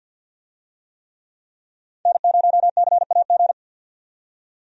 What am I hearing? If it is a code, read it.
Morse code, heard as a keyed tone, about 52 wpm, 700 Hz. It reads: N0CAG
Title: USA Call Signs - 5 Characters - 50wpm